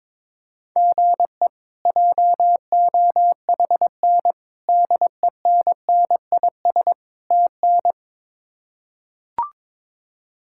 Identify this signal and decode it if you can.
Morse code, heard as a keyed tone, about 22 wpm, 700 Hz. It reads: GE JOHN DENNIS TN
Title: K1USN SST QSO Practice - 22wpm